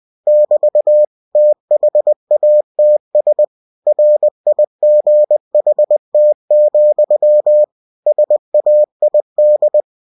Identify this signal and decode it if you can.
Morse code, heard as a keyed tone, about 20 wpm, 600 Hz. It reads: <BT> THATS RIGHT, SAID
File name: cw-books / winnie-the-pooh / winnie-20wpm-page-0046